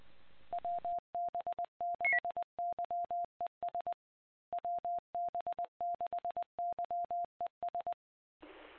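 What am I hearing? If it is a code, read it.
Morse code, heard as a keyed tone, about 20 wpm, 700 Hz. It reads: WB6YES WB6YES
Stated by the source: Morse ident of the PAPA System analog repeaters, WD6FZA.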